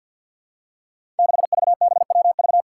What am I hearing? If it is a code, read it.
Morse code, heard as a keyed tone, about 53 wpm, 700 Hz. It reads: 6VBWV
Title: Five Challenging Letter-Number Combinations 50wpm